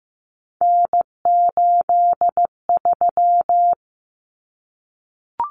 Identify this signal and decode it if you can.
Morse code, heard as a keyed tone, about 15 wpm, 700 Hz. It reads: N83
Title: Common Cut Numbers - 3-Digit 15wpm